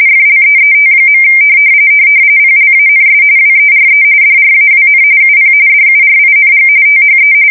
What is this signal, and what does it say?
RTTY, about 45.45 baud. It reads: FLEET. PLANS WERE DRAWN UP DURING THE